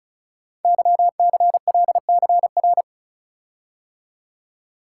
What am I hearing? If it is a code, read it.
Morse code, heard as a keyed tone, about 36 wpm, 690 Hz. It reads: YCLCR
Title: Code Groups - 5 Letters - 35wpm